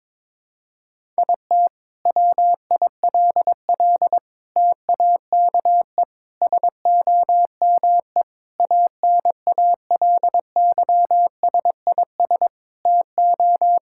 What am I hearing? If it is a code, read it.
Morse code, heard as a keyed tone, about 22 wpm, 700 Hz. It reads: IT WILL TAKE SOME ANALYSIS TO
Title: Sentences from Top 1000 Words - Review Part 5 - 22wpm